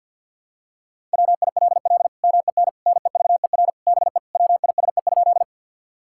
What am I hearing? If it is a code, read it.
Morse code, heard as a keyed tone, about 50 wpm, 700 Hz. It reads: WILL MEN NEVER BE WISE?
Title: Sentences from Top 1600 Words 50wpm